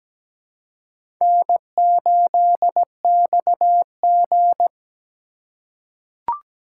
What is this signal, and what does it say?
Morse code, heard as a keyed tone, about 17 wpm, 700 Hz. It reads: N8XG
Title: USA Call Signs - 4x Character Spacing - 17wpm